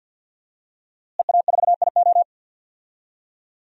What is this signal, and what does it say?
Morse code, heard as a keyed tone, about 51 wpm, 700 Hz. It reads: EA4IO
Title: DXCC-100 Call Signs - 5 Characters - Repeated 2x - 50wpm